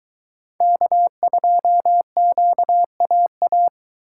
Morse code, heard as a keyed tone, about 21 wpm, 700 Hz. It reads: K2QAA